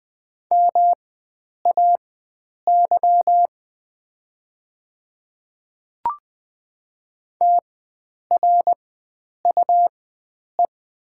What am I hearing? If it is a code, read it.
Morse code, heard as a keyed tone, about 20 wpm, spaced as for 9 wpm, 700 Hz. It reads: MAY TRUE
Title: 4x Character Spacing - Top 500 Words - 20wpm